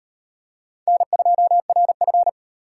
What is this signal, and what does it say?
Morse code, heard as a keyed tone, about 34 wpm, 700 Hz. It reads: N2RF